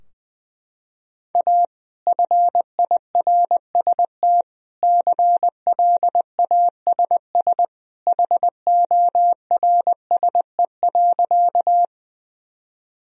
Morse code, heard as a keyed tone, about 20 wpm, 700 Hz. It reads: A FIRST CLASS HORSE.
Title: Sentences from Top 1000 Words - Review Part 2 - 20wpm